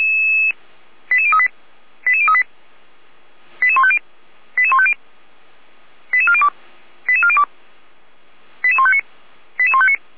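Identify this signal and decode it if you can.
Five-tone selective call (ZVEI), heard as quick runs of five tones, tones of about 70 ms. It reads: 88038 88038 88150 88150 88402 88402 88159 88159
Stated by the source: Five-tone ZVEI selective call 89701, received on SDR.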